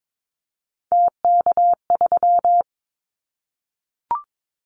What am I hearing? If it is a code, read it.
Morse code, heard as a keyed tone, about 22 wpm, 700 Hz. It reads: TK3
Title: Callsign Prefixes for Top 100 DXCC - 22wpm